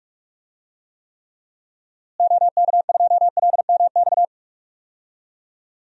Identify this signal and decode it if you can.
Morse code, heard as a keyed tone, about 46 wpm, 700 Hz. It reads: OK2LMX
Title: DXCC-100 Call Signs - 6 Characters - Repeated 3x - 45wpm